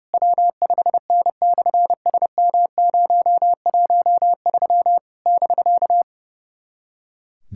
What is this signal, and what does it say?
Morse code, heard as a keyed tone, about 30 wpm, 700 Hz. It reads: W5N/SM013 <BK>
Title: SOTA QSO Practice 30wpm